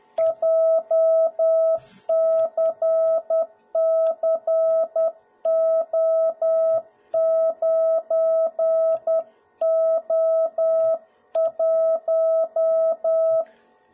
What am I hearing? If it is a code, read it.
Morse code, heard as a keyed tone, about 10 wpm, 650 Hz. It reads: JCCO9O1
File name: jh8cbh / 2021cw / cw01b